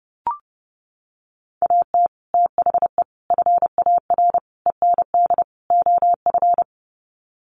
Morse code, heard as a keyed tone, about 28 wpm, 700 Hz. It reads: AT THE FAR END OF
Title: Sets of 5 Words 30wpm